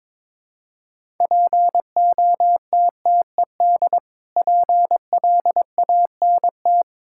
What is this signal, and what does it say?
Morse code, heard as a keyed tone, about 22 wpm, 690 Hz. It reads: POTTED PLANT
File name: Difficult Words with Hints - 22wpm